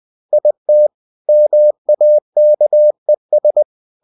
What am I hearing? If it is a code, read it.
Morse code, heard as a keyed tone, about 20 wpm, 600 Hz. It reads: IT MAKES
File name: cw-books / winnie-the-pooh / winnie-20wpm-page-0009